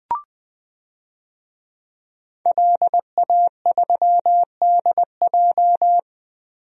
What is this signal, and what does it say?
Morse code, heard as a keyed tone, about 21 wpm, 700 Hz. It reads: LA3DJ
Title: DXCC-100 Call Signs - 5 Characters - Repeated 2x - 20wpm